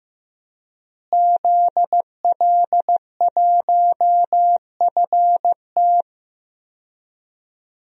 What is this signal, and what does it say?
Morse code, heard as a keyed tone, about 16 wpm, 700 Hz. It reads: ZL1FT
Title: DXCC-100 Call Signs - 5 Characters - Repeated 2x - 15wpm